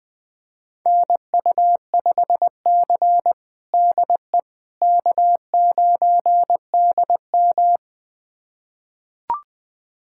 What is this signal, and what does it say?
Morse code, heard as a keyed tone, about 21 wpm, 700 Hz. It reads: NU5C DE K9DM
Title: USA Callsign DE Callsign - 4 Characters - 20wpm